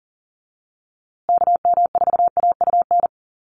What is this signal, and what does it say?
Morse code, heard as a keyed tone, about 40 wpm, 700 Hz. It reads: KM4AUN
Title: USA Call Signs Repeated 2x 40wpm